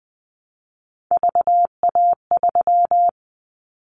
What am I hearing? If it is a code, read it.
Morse code, heard as a keyed tone, about 20 wpm, 700 Hz. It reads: VA3
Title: Callsign Prefixes for Top 25 DXCC - Repeated 3x - 20wpm